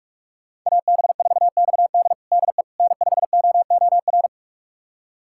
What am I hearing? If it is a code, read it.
Morse code, heard as a keyed tone, about 46 wpm, 700 Hz. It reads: AB4XD DE N5OOR
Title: USA Callsign DE Callsign - 5 Characters - 45wpm